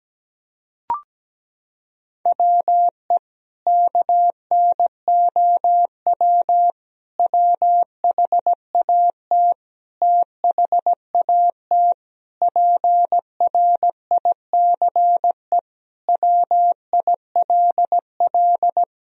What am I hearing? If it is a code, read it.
Morse code, heard as a keyed tone, about 17 wpm, 700 Hz. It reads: WE KNOW WHAT THAT PRICE WILL
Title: Sentences from Top 1000 Words - Review Part 4 - 17wpm